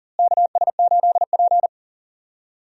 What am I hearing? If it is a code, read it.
Morse code, heard as a keyed tone, about 40 wpm, 700 Hz. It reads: KS8P
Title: USA Call Signs - 4 Characters - Repeated 3x - 40wpm